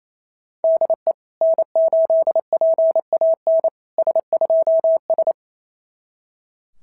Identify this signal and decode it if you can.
Morse code, heard as a keyed tone, about 28 wpm, 660 Hz. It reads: DE N8PAN S2S
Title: SOTA QSO Practice 28wpm